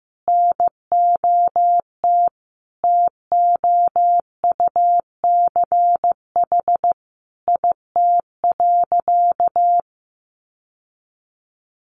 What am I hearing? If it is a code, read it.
Morse code, heard as a keyed tone, about 15 wpm, 700 Hz. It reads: NOT TOUCH IT.
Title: Sentences from Top 1200 Words 15wpm